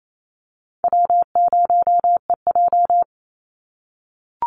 Morse code, heard as a keyed tone, about 28 wpm, 700 Hz. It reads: W0EJ
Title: Easy Call Signs - 28wpm